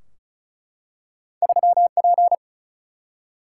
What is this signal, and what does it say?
Morse code, heard as a keyed tone, about 36 wpm, 700 Hz. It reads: 3P
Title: Single-Letter Single-Number 35wpm